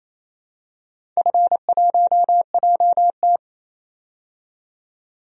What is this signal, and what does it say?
Morse code, heard as a keyed tone, about 28 wpm, 700 Hz. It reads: F1JT
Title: DXCC-100 Call Signs - 4 Characters - 28wpm